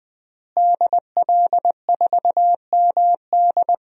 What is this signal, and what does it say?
Morse code, heard as a keyed tone, about 20 wpm, 700 Hz. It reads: DL4MD
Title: DXCC-100 Call Signs - 5 Characters - Repeated 3x - 20wpm